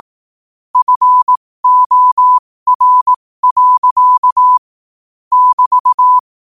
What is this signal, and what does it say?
Morse code, heard as a keyed tone, about 18 wpm, spaced as for 15 wpm, 1000 Hz. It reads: FOR. <BT>
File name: qotdfiles / 15wpm